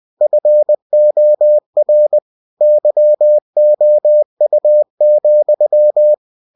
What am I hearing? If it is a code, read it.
Morse code, heard as a keyed tone, about 20 wpm, 600 Hz. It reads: FOR YOU,
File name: cw-books / winnie-the-pooh / winnie-20wpm-page-0088